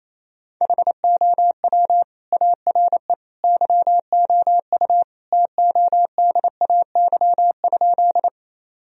Morse code, heard as a keyed tone, about 28 wpm, 700 Hz. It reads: HOW ARE YOU TODAY?
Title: Sentences from Top 1000 Words - Review Part 4 - 28wpm